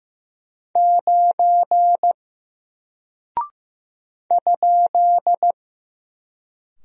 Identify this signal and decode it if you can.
Morse code, heard as a keyed tone, about 15 wpm, 700 Hz. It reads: 9 ?
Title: Single Character (Letter:Number:Punctuation:Prosign) Alt 15wpm